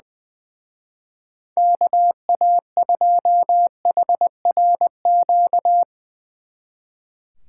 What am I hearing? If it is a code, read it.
Morse code, heard as a keyed tone, about 22 wpm, 700 Hz. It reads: KA2HRQ
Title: Speed Racing - USA 2x3 Call Signs 20wpm